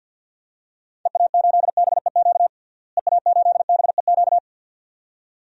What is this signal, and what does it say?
Morse code, heard as a keyed tone, about 53 wpm, 700 Hz. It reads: EA8BEQ EA8BEQ
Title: DXCC-100 Call Signs - 6 Characters - Repeated 2x - 50wpm